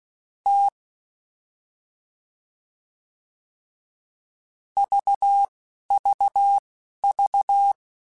Morse code, heard as a keyed tone, about 16 wpm, spaced as for 11 wpm, 800 Hz. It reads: T VVV